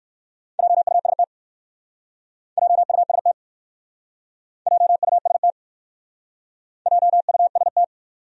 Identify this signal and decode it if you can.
Morse code, heard as a keyed tone, about 60 wpm, 700 Hz. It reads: JUST JUST JUST JUST